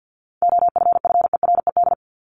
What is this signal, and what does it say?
Morse code, heard as a keyed tone, about 50 wpm, 700 Hz. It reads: OFFERED
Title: ED Suffix - Top 100 Words with Base Words 50wpm